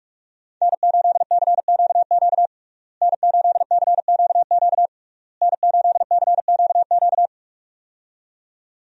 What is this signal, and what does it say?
Morse code, heard as a keyed tone, about 46 wpm, 700 Hz. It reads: N8CQQ N8CQQ N8CQQ
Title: USA Call Signs - 5 Characters - Repeated 3x - 45wpm